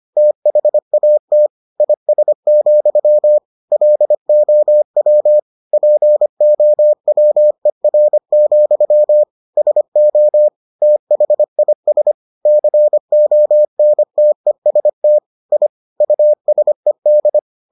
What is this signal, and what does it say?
Morse code, heard as a keyed tone, about 25 wpm, 600 Hz. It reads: THAT IS, LOW POWER, SO THIS CONTEST I USED